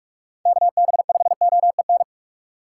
Morse code, heard as a keyed tone, about 42 wpm, 700 Hz. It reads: KB5OEN
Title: USA Call Signs - 6 Characters - 6x Character Spacing - 45wpm